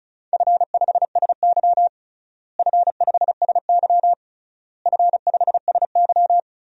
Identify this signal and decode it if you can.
Morse code, heard as a keyed tone, about 36 wpm, 700 Hz. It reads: F5SY F5SY F5SY
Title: DXCC-100 Call Signs - 4 Characters - Repeated 3x - 35wpm